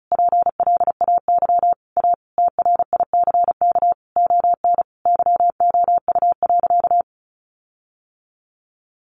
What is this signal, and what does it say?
Morse code, heard as a keyed tone, about 34 wpm, 700 Hz. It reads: PLAY A TRICK ON YOU.
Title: Sentences from Top 1500 Words 35wpm